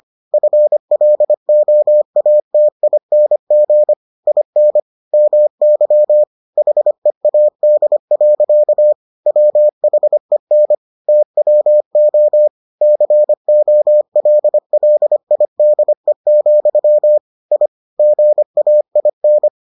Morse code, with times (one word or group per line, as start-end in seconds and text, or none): 0.3-3.9 FLOATING
4.3-4.8 IN
5.1-6.2 MY
6.6-8.9 HEAD.
9.3-10.8 WHEN
11.1-12.5 TWO
12.8-17.2 COLLIDE,
17.5-17.7 I
18.0-19.5 GAIN